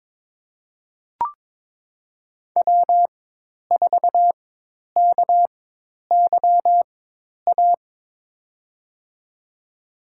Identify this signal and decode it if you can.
Morse code, heard as a keyed tone, about 22 wpm, spaced as for 10 wpm, 700 Hz. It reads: W4KYA
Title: USA Call Signs - 5 Characters - 4x Character Spacing - 22wpm